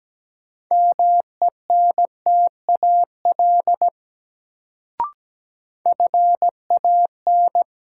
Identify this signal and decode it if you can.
Morse code, heard as a keyed tone, about 17 wpm, 700 Hz. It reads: MENTAL FAN
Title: Sentences from Top 1000 Words - Review Part 4 - New Word Drill - 17wpm